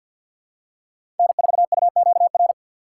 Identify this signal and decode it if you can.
Morse code, heard as a keyed tone, about 52 wpm, 700 Hz. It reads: N4UQR
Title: USA Call Signs - 5 Characters - 4x Character Spacing - 50wpm